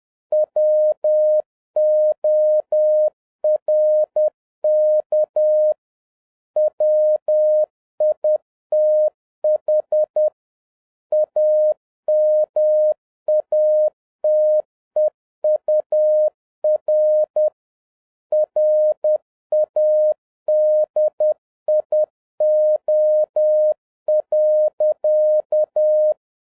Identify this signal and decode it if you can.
Morse code, heard as a keyed tone, about 10 wpm, 610 Hz. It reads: WORK WITH AMATEUR RADIO.